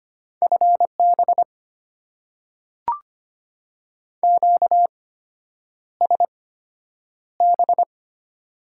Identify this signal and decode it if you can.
Morse code, heard as a keyed tone, about 25 wpm, spaced as for 9 wpm, 700 Hz. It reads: FB QSB
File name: Top 90 QSO Elements - 8x Character Spacing - 25wpm